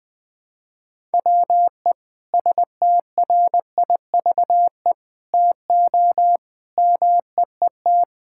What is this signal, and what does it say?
Morse code, heard as a keyed tone, about 21 wpm, 700 Hz. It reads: WE STRIVE TO MEET
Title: Sentences with 5-Word Phrases - 20wpm